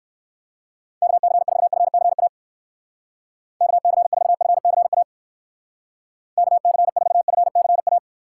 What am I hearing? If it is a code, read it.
Morse code, heard as a keyed tone, about 55 wpm, 700 Hz. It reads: KC4FCA KC4FCA KC4FCA
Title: Speed Racing - USA 2x3 Call Signs 45wpm